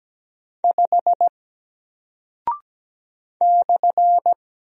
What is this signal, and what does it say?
Morse code, heard as a keyed tone, about 17 wpm, 700 Hz. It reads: 5 /
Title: Single Character (Letter:Number:Punctuation:Prosign) Alt 17wpm